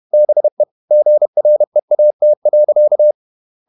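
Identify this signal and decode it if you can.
Morse code, heard as a keyed tone, about 31 wpm, 600 Hz. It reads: BE GREAT.
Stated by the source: Morse code at 30 wpm, the novel Never by Ken Follett in English, from CW med Gnister.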